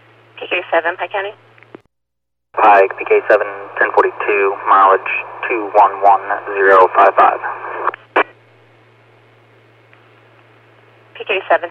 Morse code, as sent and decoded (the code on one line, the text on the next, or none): none